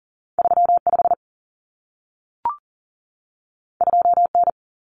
Morse code, heard as a keyed tone, about 40 wpm, 700 Hz. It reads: 35 2N
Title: Common Cut Numbers - 2-Digit 40wpm